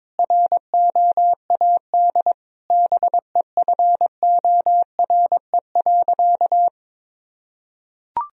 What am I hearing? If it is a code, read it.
Morse code, heard as a keyed tone, about 22 wpm, 700 Hz. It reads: ROAD BEFORE.
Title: Sentences from Top 1000 Words - Review Part 2 - 22wpm